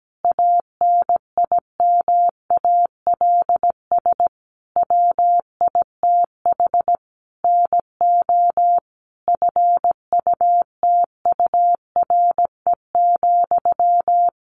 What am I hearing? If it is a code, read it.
Morse code, heard as a keyed tone, about 17 wpm, 700 Hz. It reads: ANIMALS WITH NO FUTURE,
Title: A Sound Of Thunder 17wpm